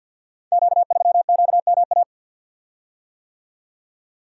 Morse code, heard as a keyed tone, about 47 wpm, 700 Hz. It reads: Q3QMA
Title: Code Groups - 5 Letter:Numbers - 50wpm